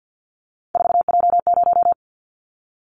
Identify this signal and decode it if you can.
Morse code, heard as a keyed tone, about 50 wpm, 700 Hz. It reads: 4J0
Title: Callsign Prefixes for Top 100 DXCC - 50wpm